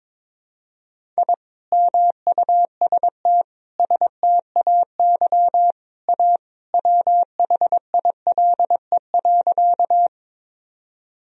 Morse code, heard as a keyed tone, about 22 wpm, 700 Hz. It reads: I MUST STAY A WHILE.
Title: Sentences from Top 1000 Words - Review Part 1 - 22wpm